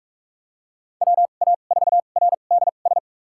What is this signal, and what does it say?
Morse code, heard as a keyed tone, about 41 wpm, spaced as for 31 wpm, 700 Hz. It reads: WA4RDS
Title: USA Call Signs - 2x Character Spacing - 45wpm